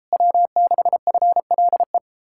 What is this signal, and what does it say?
Morse code, heard as a keyed tone, about 37 wpm, 700 Hz. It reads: W6FLE